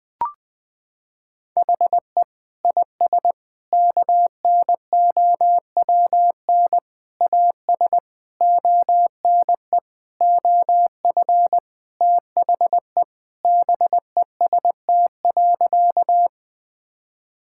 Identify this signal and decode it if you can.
Morse code, heard as a keyed tone, about 20 wpm, 700 Hz. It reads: HE IS KNOWN AS ONE OF THE BEST.
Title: Sentences with 4-Word Phrases - 20wpm